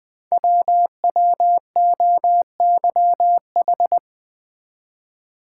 Morse code, heard as a keyed tone, about 20 wpm, 700 Hz. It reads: WWOYH
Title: Code Groups - 5 Letters - 20wpm